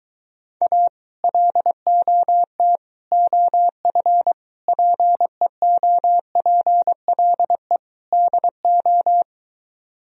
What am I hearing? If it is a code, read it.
Morse code, heard as a keyed tone, about 23 wpm, 700 Hz. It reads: A LOT OF PEOPLE DO